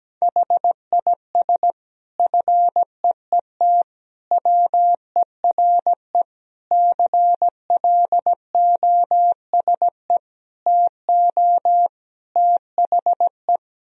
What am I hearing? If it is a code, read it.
Morse code, heard as a keyed tone, about 17 wpm, 700 Hz. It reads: HIS FEET WERE CLOSE TO THE
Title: Sentences from Top 1000 Words - Review Part 1 - 17wpm